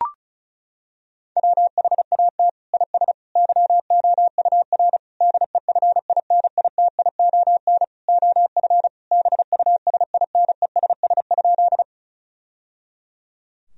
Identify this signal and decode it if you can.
Morse code, heard as a keyed tone, about 36 wpm, 700 Hz. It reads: WHAT IS YOUR DEFINITION OF BUSINESS?
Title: Sentences from Top 1000 Words - Review Part 4 - 35wpm